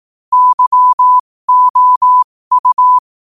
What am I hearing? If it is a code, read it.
Morse code, heard as a keyed tone, about 18 wpm, spaced as for 15 wpm, 1000 Hz. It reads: YOU
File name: qotdfiles / 15wpm